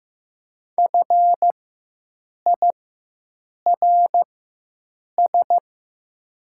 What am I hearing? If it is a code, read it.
Morse code, heard as a keyed tone, about 15 wpm, spaced as for 7 wpm, 700 Hz. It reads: FIRS
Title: 4x Character Spacing - Top 300 Words - 15wpm